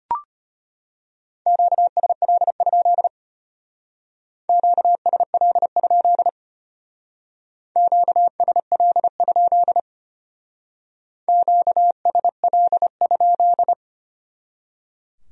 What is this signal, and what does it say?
Morse code, heard as a keyed tone, about 32 wpm, 700 Hz. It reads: QSL? QSL? QSL? QSL?